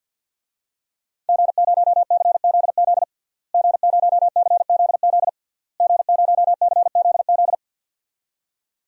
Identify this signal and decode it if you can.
Morse code, heard as a keyed tone, about 52 wpm, 700 Hz. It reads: G0CZZ G0CZZ G0CZZ
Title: DXCC-100 Call Signs - 5 Characters - Repeated 3x - 50wpm